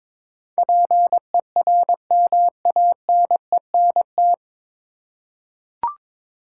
Morse code, heard as a keyed tone, about 22 wpm, 700 Hz. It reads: PERMANENT
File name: 9 Letter Words - First 3 Letters Given - 22wpm